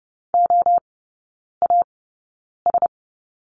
Morse code, heard as a keyed tone, about 30 wpm, spaced as for 9 wpm, 690 Hz. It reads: OAS
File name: Lesson 17e - Letter u - Three Characters 30wpm